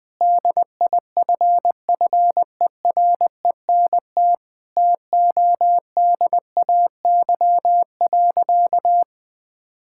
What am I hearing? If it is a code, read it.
Morse code, heard as a keyed tone, about 20 wpm, 700 Hz. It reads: DIFFERENT TODAY.
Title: Sentences from Top 1600 Words 20wpm